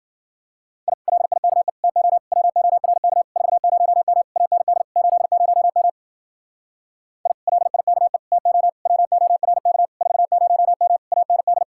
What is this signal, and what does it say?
Morse code, heard as a keyed tone, about 51 wpm, 700 Hz. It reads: I LIKE TO WORK 40M AND 80M I LIKE TO WORK 40M AND